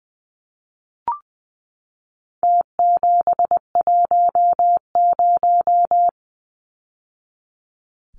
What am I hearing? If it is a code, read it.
Morse code, heard as a keyed tone, about 20 wpm, 700 Hz. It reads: T710
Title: Numbers - Contest Style - 4-Digit 20wpm